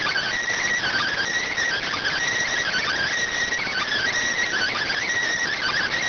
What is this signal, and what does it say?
Emergency Alert System (SAME) tones (header bursts), heard as harsh buzzing bursts. It is heard from 0.8 to 5.0 s.